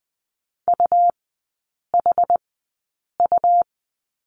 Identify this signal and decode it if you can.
Morse code, heard as a keyed tone, about 20 wpm, spaced as for 8 wpm, 700 Hz. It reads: UHU
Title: 100 Lesson 17e - Letter u - Three Characters 20wpm